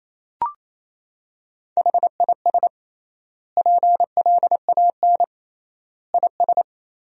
Morse code, heard as a keyed tone, about 27 wpm, 700 Hz. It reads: HIS PLAN IS
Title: 3x Word-Spacing - Sentences from Top 500 Words - 28wpm